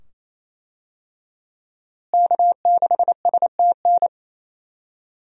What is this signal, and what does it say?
Morse code, heard as a keyed tone, about 28 wpm, 700 Hz. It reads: K6STN
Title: USA Call Signs - 5 Characters - Repeated 2x - 28wpm